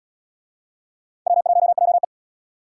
Morse code, heard as a keyed tone, about 70 wpm, 700 Hz. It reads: W1JE